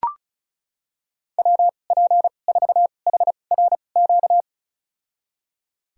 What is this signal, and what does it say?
Morse code, heard as a keyed tone, about 36 wpm, spaced as for 25 wpm, 700 Hz. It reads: WP4HRQ
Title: USA Call Signs - 6 Characters - 2x Character Spacing - 35wpm